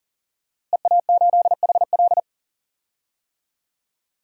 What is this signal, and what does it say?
Morse code, heard as a keyed tone, about 40 wpm, 700 Hz. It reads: EA8HL
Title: DXCC-100 Call Signs - 5 Characters - Repeated 2x - 40wpm